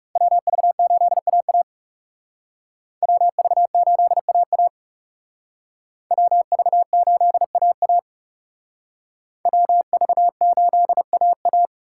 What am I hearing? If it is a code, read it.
Morse code, heard as a keyed tone, about 36 wpm, 700 Hz. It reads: WV8AA WV8AA WV8AA WV8AA
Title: Speed Racing - USA 2x2 Call Signs 30wpm